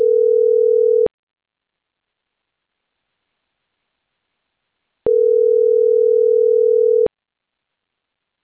Ringback tone (North America), 440 + 480 Hz.